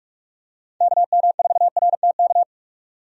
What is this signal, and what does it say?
Morse code, heard as a keyed tone, about 44 wpm, 700 Hz. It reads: KM4RTK
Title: USA Call Signs - 6 Characters - 45wpm